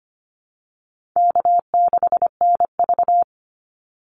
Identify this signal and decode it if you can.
Morse code, heard as a keyed tone, about 25 wpm, 700 Hz. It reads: K6NV
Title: Easy Call Signs Repeated 3x - 25wpm